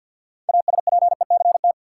Morse code, heard as a keyed tone, about 53 wpm, 700 Hz. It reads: ASPECT
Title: Six Letter Words - 50wpm